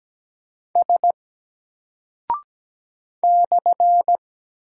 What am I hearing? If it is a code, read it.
Morse code, heard as a keyed tone, about 17 wpm, 700 Hz. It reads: S /
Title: Single Character (Letter:Number:Punctuation:Prosign) 17wpm